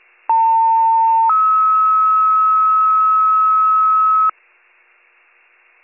Two-tone page, heard as a one-second tone, then a three-second tone, about 900 then 1290 Hz, with noise. It is heard from 0.3 to 4.3 s.